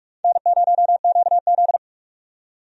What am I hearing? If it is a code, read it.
Morse code, heard as a keyed tone, about 44 wpm, 700 Hz. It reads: N0QZ
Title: USA Call Signs - 4 Characters - 45wpm